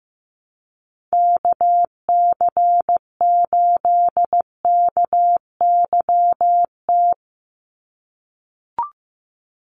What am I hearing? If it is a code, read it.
Morse code, heard as a keyed tone, about 15 wpm, 700 Hz. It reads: KC8KYT
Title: USA Call Signs 15wpm